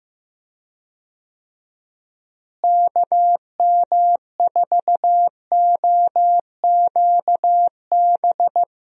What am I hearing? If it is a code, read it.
Morse code, heard as a keyed tone, about 15 wpm, 700 Hz. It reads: KM4OQB
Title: USA Call Signs - 6 Characters - Repeated 2x - 15wpm